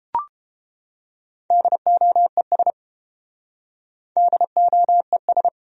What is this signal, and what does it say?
Morse code, heard as a keyed tone, about 30 wpm, 700 Hz. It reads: DOES DOES